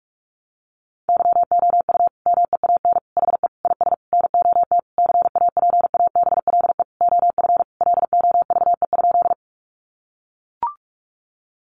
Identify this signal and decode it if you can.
Morse code, heard as a keyed tone, about 44 wpm, 700 Hz. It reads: YOU MEAN HE IS NOT CAPABLE OF LOVE?
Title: Sentences from Top 1200 Words 45wpm